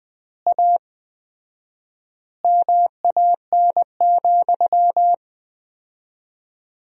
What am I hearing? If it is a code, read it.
Morse code, heard as a keyed tone, about 20 wpm, 700 Hz. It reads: A MAN,
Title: 4x Word-Spacing - Sentences from Top 200 Words - 20wpm